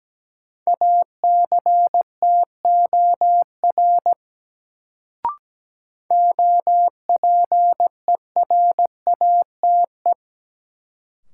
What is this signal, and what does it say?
Morse code, heard as a keyed tone, about 17 wpm, 700 Hz. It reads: ACTOR OPERATE
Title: Sentences from Top 1500 Words - New Word Drill Alt - 17wpm